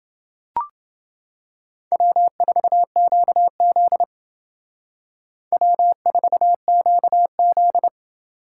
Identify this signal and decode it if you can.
Morse code, heard as a keyed tone, about 28 wpm, 700 Hz. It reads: W4QZ W4QZ